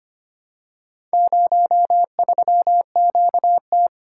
Morse code, heard as a keyed tone, about 25 wpm, 700 Hz. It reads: 03QT
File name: practice / 25wpm / Four Letter-Number Combinations - 25wpm